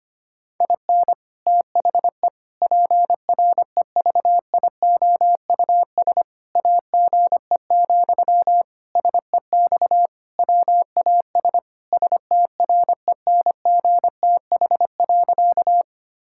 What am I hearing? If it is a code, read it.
Morse code, heard as a keyed tone, about 25 wpm, 700 Hz. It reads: IN THE PREVIOUS AGE, SEX WAS STRENGTH.
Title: Sentences from Top 1600 Words 25wpm